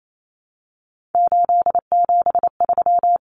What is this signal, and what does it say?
Morse code, heard as a keyed tone, about 28 wpm, 700 Hz. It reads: 873
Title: Numbers - 3-Digit 28wpm